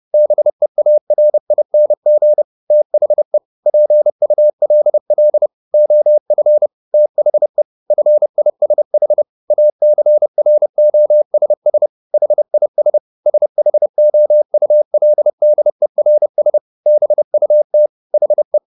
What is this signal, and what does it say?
Morse code, heard as a keyed tone, about 30 wpm, 600 Hz. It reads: BEARING THE PULL OF THE FISH ACROSS HIS SHOULDERS BUT HE